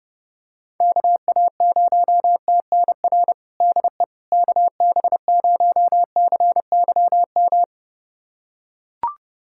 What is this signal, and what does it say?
Morse code, heard as a keyed tone, about 30 wpm, 700 Hz. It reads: KA0TNR DE KB0CYM
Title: USA Callsign DE Callsign 30wpm